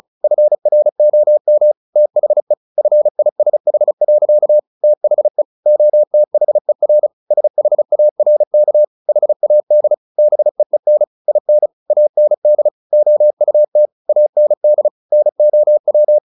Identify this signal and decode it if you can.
Morse code, heard as a keyed tone, about 34 wpm, 600 Hz. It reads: FROM THE FISH. THE OTHER SHARK HAD BEEN IN AND OUT AND NOW
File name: the old man and the sea 2 35wpm